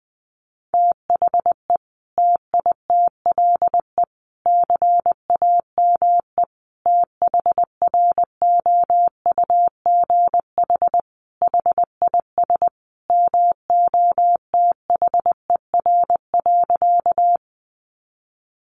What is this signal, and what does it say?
Morse code, heard as a keyed tone, about 20 wpm, 700 Hz. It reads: THE TITLE CAME THROUGH HIS MOTHER.
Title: Sentences from Top 1200 Words 20wpm